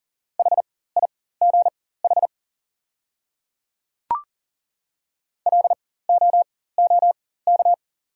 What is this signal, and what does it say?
Morse code, heard as a keyed tone, about 37 wpm, spaced as for 18 wpm, 700 Hz. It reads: HIGH LOOK
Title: 4x Character Spacing - Top 500 Words - 40wpm